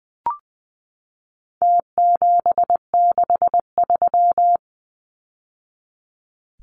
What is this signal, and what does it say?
Morse code, heard as a keyed tone, about 20 wpm, 700 Hz. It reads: T763
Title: Numbers - Contest Style - 4-Digit 20wpm